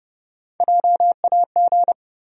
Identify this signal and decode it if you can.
Morse code, heard as a keyed tone, about 30 wpm, 700 Hz. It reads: JAG
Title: USA Call Sign Postfixes - 3 Letters 30wpm